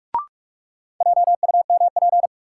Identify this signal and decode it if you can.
Morse code, heard as a keyed tone, about 46 wpm, 700 Hz. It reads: JUMP
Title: Sentences from Top 1300 Words - New Word Drill Alt - 45wpm